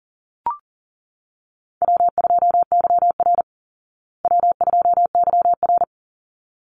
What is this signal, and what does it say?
Morse code, heard as a keyed tone, about 40 wpm, 700 Hz. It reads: W2YR W2YR
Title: Easy Call Signs Repeated 2x - 40wpm